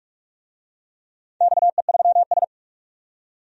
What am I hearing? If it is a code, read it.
Morse code, heard as a keyed tone, about 42 wpm, 700 Hz. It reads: XE3S
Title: DXCC-100 Call Signs - 4 Characters - 45wpm